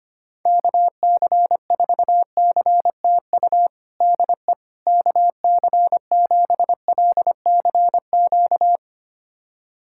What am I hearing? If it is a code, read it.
Morse code, heard as a keyed tone, about 25 wpm, 700 Hz. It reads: KC4CTU DE KC7LCQ
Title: USA Callsign DE Callsign - 6 Characters - 25wpm